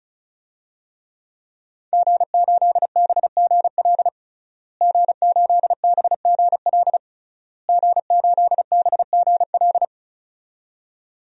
Morse code, heard as a keyed tone, about 34 wpm, 700 Hz. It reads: G8BGL G8BGL G8BGL
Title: DXCC-100 Call Signs - 5 Characters - Repeated 3x - 35wpm